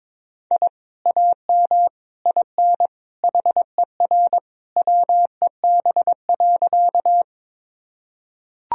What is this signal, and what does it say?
Morse code, heard as a keyed tone, about 22 wpm, 700 Hz. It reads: I AM IN HER WEB.
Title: Sentences from Top 1000 Words - Review Part 4 - 22wpm